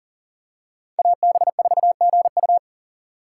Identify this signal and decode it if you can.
Morse code, heard as a keyed tone, about 40 wpm, 700 Hz. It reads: AB4GU
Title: USA Call Signs - 5 Characters - 3x Character Spacing - 40wpm